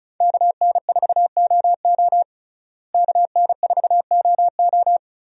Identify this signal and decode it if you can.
Morse code, heard as a keyed tone, about 36 wpm, 700 Hz. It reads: KN4OO KN4OO
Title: USA Call Signs - 5 Characters - Repeated 3x - 35wpm